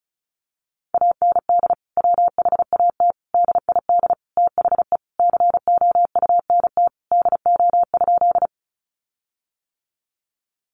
Morse code, heard as a keyed tone, about 34 wpm, 700 Hz. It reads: AND WHAT DID THE COUNT DO?
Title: Sentences from Top 1500 Words 35wpm